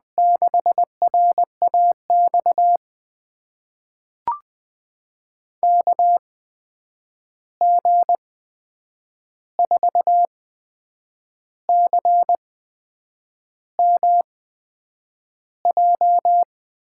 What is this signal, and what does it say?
Morse code, heard as a keyed tone, about 20 wpm, spaced as for 8 wpm, 700 Hz. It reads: 6RAX KG4CMJ
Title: USA Call Signs - 6 Characters - 8x Character Spacing - 20wpm